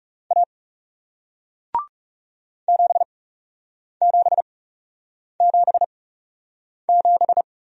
Morse code, heard as a keyed tone, about 38 wpm, spaced as for 8 wpm, 700 Hz. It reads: A 7777